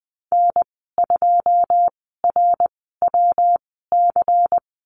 Morse code, heard as a keyed tone, about 20 wpm, spaced as for 15 wpm, 700 Hz. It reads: N2RWC
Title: 154 Lesson 25g - Letter p - Call Signs 20wpm